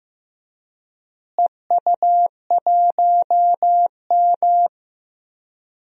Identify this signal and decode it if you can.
Morse code, heard as a keyed tone, about 15 wpm, 700 Hz. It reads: EU1M
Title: DXCC-100 Call Signs - 4 Characters - Repeated 2x - 15wpm